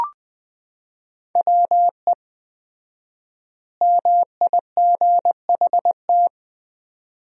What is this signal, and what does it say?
Morse code, heard as a keyed tone, about 21 wpm, 700 Hz. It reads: WE MIGHT
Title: 4x Word-Spacing - Sentences from Top 300 Words - 20wpm